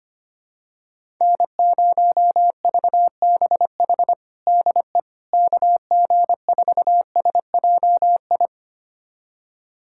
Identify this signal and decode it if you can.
Morse code, heard as a keyed tone, about 25 wpm, 700 Hz. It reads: N0VBH DE KG4SJI
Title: USA Callsign DE Callsign 25wpm